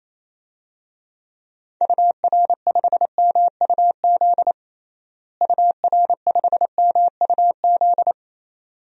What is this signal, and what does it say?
Morse code, heard as a keyed tone, about 28 wpm, 700 Hz. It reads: UR5MUZ UR5MUZ
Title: DXCC-100 Call Signs - 6 Characters - Repeated 2x - 28wpm